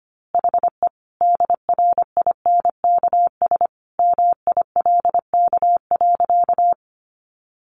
Morse code, heard as a keyed tone, about 25 wpm, 700 Hz. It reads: HE DRINKS MILK.
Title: Easy Phrases 25wpm